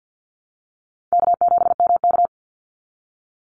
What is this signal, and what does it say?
Morse code, heard as a keyed tone, about 50 wpm, 700 Hz. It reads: K7MK
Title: NA SOTA Activator Callsigns 50wpm